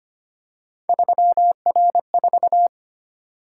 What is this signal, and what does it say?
Morse code, heard as a keyed tone, about 29 wpm, 700 Hz. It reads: 3R4
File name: Speed-Racing - Three Letter-Number Combinations - 25wpm